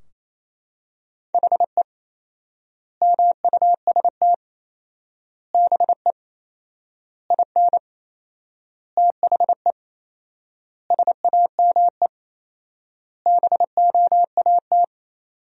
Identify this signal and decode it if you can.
Morse code, heard as a keyed tone, about 28 wpm, 700 Hz. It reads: HE MUST BE IN THE SAME BOAT
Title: 4x Word-Spacing - Sentences from Top 500 Words - 28wpm